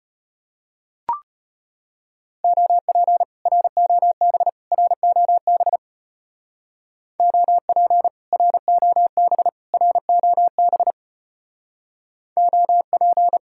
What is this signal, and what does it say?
Morse code, heard as a keyed tone, about 30 wpm, 700 Hz. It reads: OP ROB ROB OP ROB ROB OP
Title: Speed-Racing Common QSO Phrases - 25wpm